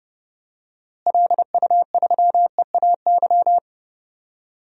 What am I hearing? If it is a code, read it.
Morse code, heard as a keyed tone, about 31 wpm, 700 Hz. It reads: LU3EAY
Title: DXCC-100 Call Signs - 6 Characters - Repeated 3x - 30wpm